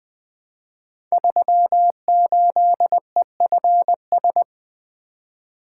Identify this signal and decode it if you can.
Morse code, heard as a keyed tone, about 20 wpm, 700 Hz. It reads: 38EFS
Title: Code Groups - 5 Letter:Numbers - 20wpm